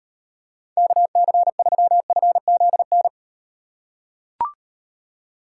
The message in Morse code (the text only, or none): KC3FZN